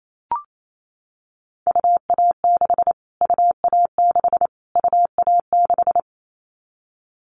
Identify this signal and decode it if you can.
Morse code, heard as a keyed tone, about 28 wpm, 700 Hz. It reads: UA6 UA6 UA6
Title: Callsign Prefixes for Top 25 DXCC - Repeated 3x - 28wpm